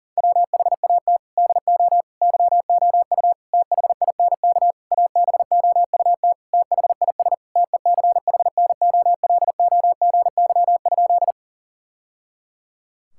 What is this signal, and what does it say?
Morse code, heard as a keyed tone, about 40 wpm, 700 Hz. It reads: WHAT DO YOU THINK ABOUT THIS TECHNOLOGY?
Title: Sentences from Top 1000 Words - Review Part 5 - 40wpm